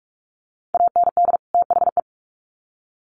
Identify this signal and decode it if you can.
Morse code, heard as a keyed tone, about 44 wpm, 700 Hz. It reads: AND THE
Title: Sets of 2 Words 45wpm